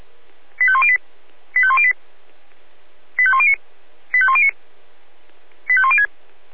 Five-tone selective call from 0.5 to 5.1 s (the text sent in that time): 85108 85108 85109 85109